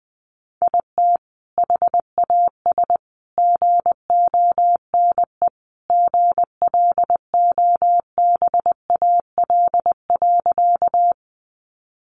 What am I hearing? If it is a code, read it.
Morse code, heard as a keyed tone, about 20 wpm, 700 Hz. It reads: IT HAS GONE GLOBAL.
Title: Sentences from Top 1200 Words 20wpm